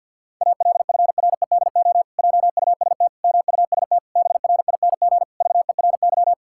Morse code, heard as a keyed tone, about 49 wpm, 700 Hz. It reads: ALFREDO JUST MUST BRING VERY
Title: Pangram Challenge v2 - 50wpm